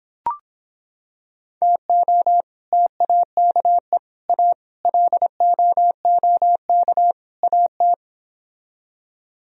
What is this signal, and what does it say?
Morse code, heard as a keyed tone, about 26 wpm, 700 Hz. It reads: TO TAKE A LOOK AT